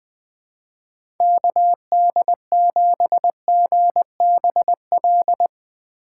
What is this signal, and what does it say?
Morse code, heard as a keyed tone, about 20 wpm, 700 Hz. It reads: KD7GBL
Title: USA Call Signs - 6 Characters - 3x Character Spacing - 20wpm